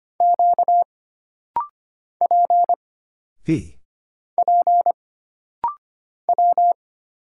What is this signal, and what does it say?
Morse code, heard as a keyed tone, about 25 wpm, 700 Hz. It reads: Q P P W
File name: Single Character (Letter:Number:Punctuation:Prosign) - Rapid-Fire - 25wpm